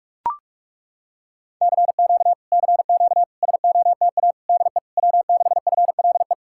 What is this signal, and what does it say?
Morse code, heard as a keyed tone, about 40 wpm, 700 Hz. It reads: CQ CQ SOTA DE W6FLE